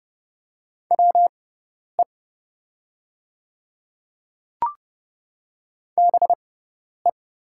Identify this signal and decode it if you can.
Morse code, heard as a keyed tone, about 30 wpm, spaced as for 10 wpm, 690 Hz. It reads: WE BE
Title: 6x Character Spacing - Top 100 Words - 30wpm